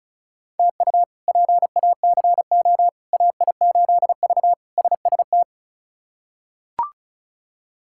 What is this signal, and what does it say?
Morse code, heard as a keyed tone, about 34 wpm, 700 Hz. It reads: TU PACO AI8V SST
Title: K1USN SST QSO Practice - 35wpm